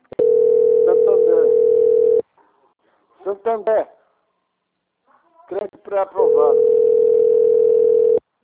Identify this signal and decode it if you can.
Ringback tone (North America), 440 + 480 Hz, with speech.
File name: out- - - - - .45816